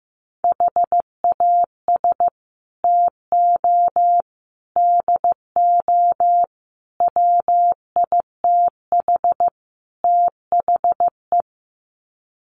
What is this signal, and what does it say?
Morse code, heard as a keyed tone, about 15 wpm, 700 Hz. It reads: HAS TO DO WITH THE
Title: Sets of 5 Words 15wpm